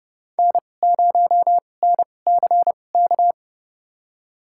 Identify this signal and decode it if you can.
Morse code, heard as a keyed tone, about 30 wpm, spaced as for 22 wpm, 700 Hz. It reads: N0NCK
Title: USA Call Signs - 5 Characters - 2x Character Spacing - 30wpm